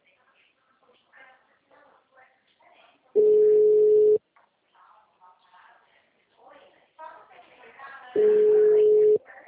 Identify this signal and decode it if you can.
Ringback tone (Europe, 425 Hz), with speech.